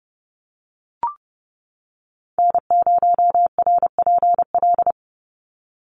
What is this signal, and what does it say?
Morse code, heard as a keyed tone, about 30 wpm, 700 Hz. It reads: N0RPL
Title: USA Call Signs 30wpm